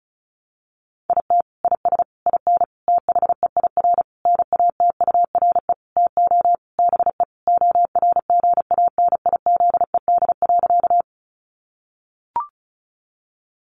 Morse code, heard as a keyed tone, about 34 wpm, 700 Hz. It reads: IT IS IN THEIR NATURE TO BE ORGANIZED.
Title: Sentences from Top 1300 Words 35wpm